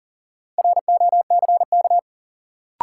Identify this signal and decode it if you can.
Morse code, heard as a keyed tone, about 38 wpm, 700 Hz. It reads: ROCK
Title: Sentences from Top 400 Words - New Word Drill - 40wpm